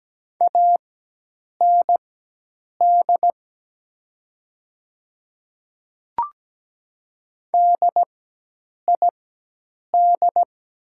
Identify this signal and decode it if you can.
Morse code, heard as a keyed tone, about 17 wpm, spaced as for 8 wpm, 700 Hz. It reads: AND DID
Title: 4x Character Spacing - Top 100 Words - 17wpm